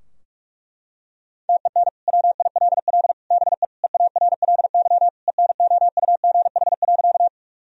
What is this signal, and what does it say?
Morse code, heard as a keyed tone, about 46 wpm, 700 Hz. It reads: TEN WILL BE EARLY ENOUGH.
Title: Sentences from Top 1000 Words - Review Part 2 - 45wpm